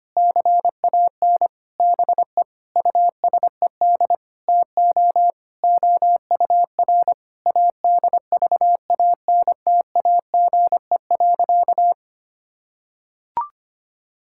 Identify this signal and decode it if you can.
Morse code, heard as a keyed tone, about 25 wpm, 700 Hz. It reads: CAN BE USED TO OUR ADVANTAGE.
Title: Sentences from Top 1000 Words - Review Part 6 - 25wpm